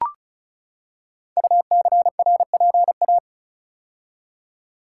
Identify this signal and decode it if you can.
Morse code, heard as a keyed tone, about 36 wpm, 700 Hz. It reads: UCRPA
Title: Code Groups - 5 Letter:Numbers - 35wpm